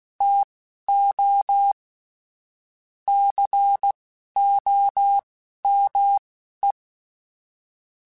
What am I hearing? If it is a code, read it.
Morse code, heard as a keyed tone, about 16 wpm, spaced as for 11 wpm, 800 Hz. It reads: TO COME
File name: QSOs / QSOs-mp3s / WA7QOX 1@10